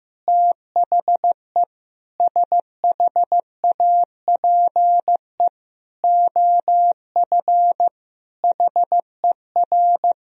Morse code, heard as a keyed tone, about 15 wpm, 700 Hz. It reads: THE SHAPE OF HER
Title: Sentences from Top 1000 Words - Review Part 3 - 15wpm